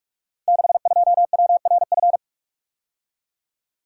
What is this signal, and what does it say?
Morse code, heard as a keyed tone, about 44 wpm, 690 Hz. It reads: 62WRF
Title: Code Groups - 5 Letter:Numbers - 45wpm